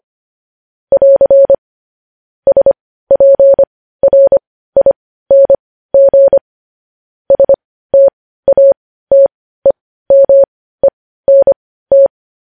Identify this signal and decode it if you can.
Morse code, heard as a keyed tone, about 25 wpm, spaced as for 15 wpm, 550 Hz. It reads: <AR> SPRING STATEMENT
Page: News Headlines in Morse Code at 15 WPM